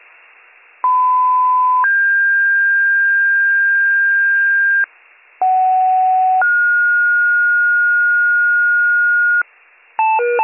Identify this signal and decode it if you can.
Two-tone page, heard as a one-second tone, then a three-second tone, about 1010 then 1600 Hz, about 750 then 1430 Hz, with noise. It is heard from 0.8 to 9.4 s.